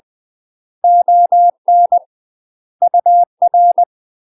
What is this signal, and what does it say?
Morse code, heard as a keyed tone, about 20 wpm, 700 Hz. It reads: ON UR
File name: UK MASTER QSOs TRAINING / 20WPM / UK MASTER QSOs 0071